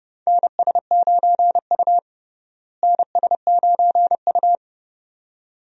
Morse code, heard as a keyed tone, about 32 wpm, 700 Hz. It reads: NS9U NS9U
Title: USA Call Signs - 4 Characters - Repeated 2x - 30wpm